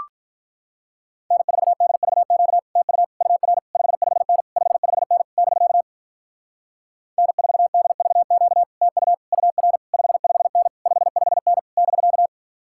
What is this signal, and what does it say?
Morse code, heard as a keyed tone, about 48 wpm, 700 Hz. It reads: N4DVQ TU UR 55N 55N <BK> N4DVQ TU UR 55N 55N <BK>